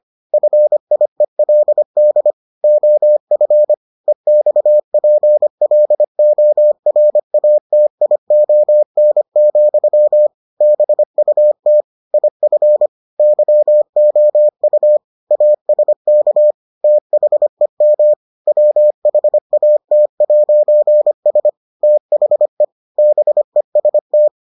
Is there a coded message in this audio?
Yes: Morse code from 0.3 to 24.3 s, heard as a keyed tone, about 25 wpm, 600 Hz. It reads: FIELD OF EXPLORATION, BUT IF YOU ASK THEM WHAT'S THE BEST